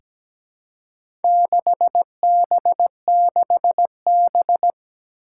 Morse code, heard as a keyed tone, about 17 wpm, 700 Hz. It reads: 6B6B
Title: Four Challenging Letter-Number Combinations 17wpm